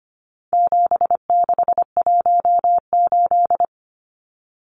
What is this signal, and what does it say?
Morse code, heard as a keyed tone, about 25 wpm, 700 Hz. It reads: 7618
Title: Numbers - 4-Digit 25wpm